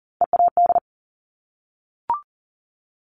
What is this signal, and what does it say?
Morse code, heard as a keyed tone, about 38 wpm, 690 Hz. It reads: EAD
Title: Top Three Letter Combinations in Words 40wpm